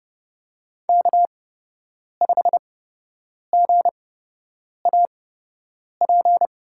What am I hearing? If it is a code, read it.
Morse code, heard as a keyed tone, about 30 wpm, spaced as for 8 wpm, 700 Hz. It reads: K5GAP
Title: USA Call Signs - 5 Characters - 8x Character Spacing - 30wpm